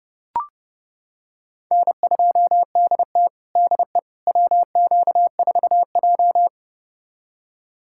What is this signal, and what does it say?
Morse code, heard as a keyed tone, about 30 wpm, 700 Hz. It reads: N2DT DE WQ4J
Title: USA Callsign DE Callsign - 4 Characters - 30wpm